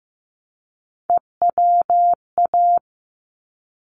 Morse code, heard as a keyed tone, about 15 wpm, 700 Hz. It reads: EWA
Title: ARRL Sections 15wpm